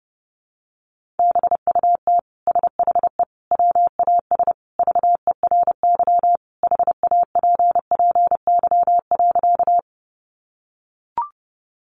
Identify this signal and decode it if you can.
Morse code, heard as a keyed tone, about 30 wpm, 700 Hz. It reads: BUT SHE WAS VERY HAPPY.
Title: Sentences from Top 1000 Words 30wpm